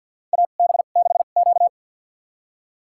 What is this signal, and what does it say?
Morse code, heard as a keyed tone, about 53 wpm, spaced as for 37 wpm, 700 Hz. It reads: AB6Q
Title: USA Call Signs - 4 Characters - 2x Character Spacing - 50wpm